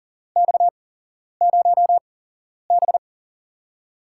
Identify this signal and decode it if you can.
Morse code, heard as a keyed tone, about 40 wpm, spaced as for 11 wpm, 700 Hz. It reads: X0B
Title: USA Call Signs - 8x Character Spacing - 40wpm